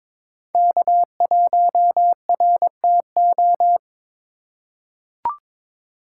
Morse code, heard as a keyed tone, about 22 wpm, 700 Hz. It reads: K1RTO
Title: USA Call Signs - 5 Characters - 3x Character Spacing - 22wpm